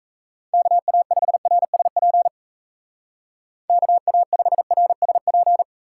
Morse code, heard as a keyed tone, about 35 wpm, 700 Hz. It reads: KA5RSP KA5RSP
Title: Speed Racing - USA Call Signs 28wpm